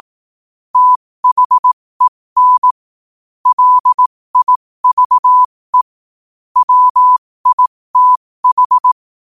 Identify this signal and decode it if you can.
Morse code, heard as a keyed tone, about 18 wpm, spaced as for 15 wpm, 1000 Hz. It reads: THEN LIVE WITH
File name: qotdfiles / 15wpm